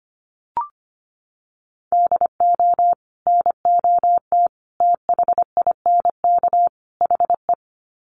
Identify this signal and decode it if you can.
Morse code, heard as a keyed tone, about 25 wpm, 700 Hz. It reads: DO NOT THINK HE
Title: Sets of 4 Words 25wpm